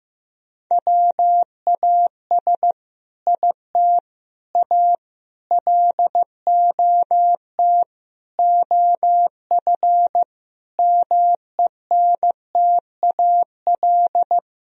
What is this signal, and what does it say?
Morse code, heard as a keyed tone, about 15 wpm, 700 Hz. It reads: WAS IT A LOT OF MENTAL
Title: Sentences from Top 1000 Words - Review Part 4 - 15wpm